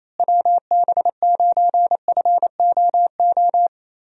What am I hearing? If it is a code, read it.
Morse code, heard as a keyed tone, about 28 wpm, 700 Hz. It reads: WB9FOO
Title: USA Call Signs - 6 Characters - Repeated 3x - 28wpm